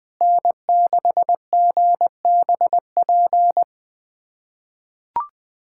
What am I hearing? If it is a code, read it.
Morse code, heard as a keyed tone, about 21 wpm, 700 Hz. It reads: N6GBP
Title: USA Callsign DE Callsign - 5 Characters - 20wpm